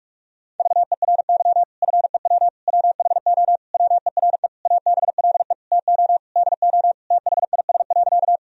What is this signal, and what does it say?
Morse code, heard as a keyed tone, about 46 wpm, 700 Hz. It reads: VERY FEW WHO WERE ABLE TO DO THIS.
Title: Sentences from Top 1000 Words - Review Part 2 - 45wpm